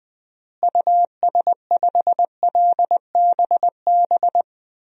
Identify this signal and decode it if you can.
Morse code, heard as a keyed tone, about 20 wpm, 700 Hz. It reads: US5LBB
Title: DXCC-100 Call Signs - 6 Characters - Repeated 2x - 20wpm